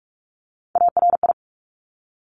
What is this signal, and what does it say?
Morse code, heard as a keyed tone, about 46 wpm, 690 Hz. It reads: ARI
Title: Top Three Letter Combinations in Words 45wpm